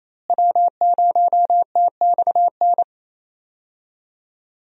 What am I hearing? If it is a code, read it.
Morse code, heard as a keyed tone, about 28 wpm, 700 Hz. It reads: W0TXN
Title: USA Call Signs - 5 Characters - Repeated 3x - 28wpm